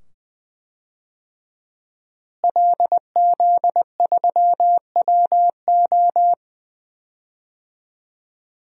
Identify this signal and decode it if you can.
Morse code, heard as a keyed tone, about 20 wpm, 700 Hz. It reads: LZ3WO
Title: DXCC-100 Call Signs - 5 Characters - Repeated 2x - 20wpm